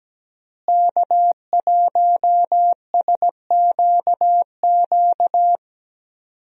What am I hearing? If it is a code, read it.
Morse code, heard as a keyed tone, about 17 wpm, 700 Hz. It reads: K1SQQ
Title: USA Call Signs - 5 Characters - Repeated 3x - 17wpm